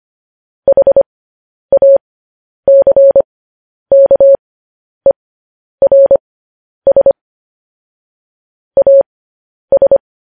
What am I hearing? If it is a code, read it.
Morse code, heard as a keyed tone, about 25 wpm, spaced as for 10 wpm, 550 Hz. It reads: HACKERS AS
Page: News Headlines in Morse Code at 10 WPM